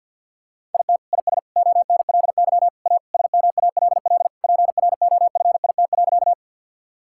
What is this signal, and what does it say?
Morse code, heard as a keyed tone, about 50 wpm, 700 Hz. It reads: IT IS ONLY A SMALL PROFIT.
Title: Sentences from Top 1000 Words - Review Part 5 - 50wpm